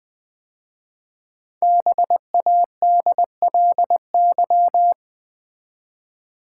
Morse code, heard as a keyed tone, about 20 wpm, 700 Hz. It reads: BADLY
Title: 5 Letter Words - First 3 Letters Given - 20wpm